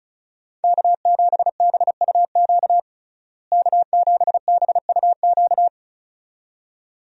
Morse code, heard as a keyed tone, about 36 wpm, 700 Hz. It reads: K7BUQ K7BUQ
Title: USA Call Signs - 5 Characters - Repeated 2x - 35wpm